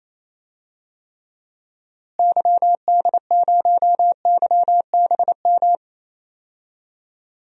Morse code, heard as a keyed tone, about 28 wpm, 700 Hz. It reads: YD0YBM
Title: DXCC-100 Call Signs - 6 Characters - Repeated 3x - 28wpm